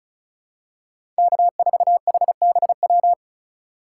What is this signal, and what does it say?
Morse code, heard as a keyed tone, about 34 wpm, 700 Hz. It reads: K4HBW
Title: USA Call Signs - 5 Characters - 2x Character Spacing - 35wpm